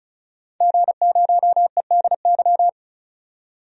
Morse code, heard as a keyed tone, about 36 wpm, 700 Hz. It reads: G0EDY
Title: DXCC-100 Call Signs - 5 Characters - 35wpm